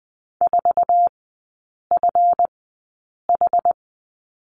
Morse code, heard as a keyed tone, about 20 wpm, spaced as for 8 wpm, 700 Hz. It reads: 4FH
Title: 184 Lesson 30e - Letter b - Three Characters 20wpm